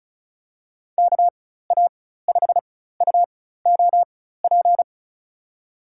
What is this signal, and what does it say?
Morse code, heard as a keyed tone, about 36 wpm, spaced as for 16 wpm, 700 Hz. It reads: KA5UOP
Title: USA Call Signs - 6 Characters - 4x Character Spacing - 35wpm